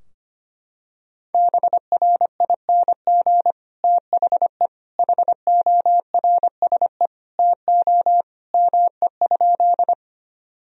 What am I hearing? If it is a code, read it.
Morse code, heard as a keyed tone, about 25 wpm, 700 Hz. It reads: BRING THE HORSE TO ME?
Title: Sentences from Top 1000 Words - Review Part 3 - 25wpm